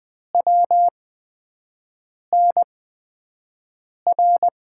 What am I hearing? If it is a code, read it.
Morse code, heard as a keyed tone, about 20 wpm, spaced as for 5 wpm, 700 Hz. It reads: WNR
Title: USA Call Signs - 6 Characters - 8x Character Spacing - 20wpm